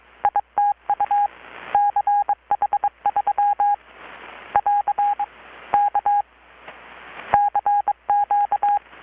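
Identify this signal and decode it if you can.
Morse code, heard as a keyed tone, about 22 wpm, 820 Hz. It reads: ITU CH3 <AR> K CQ
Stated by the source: Morse from KPH's Night of Nights XX maritime radiotelegraphy event, received in Chiapas, Mexico about 3,800 km away on a Kenwood R-600 with a grounded 10 m long-wire antenna.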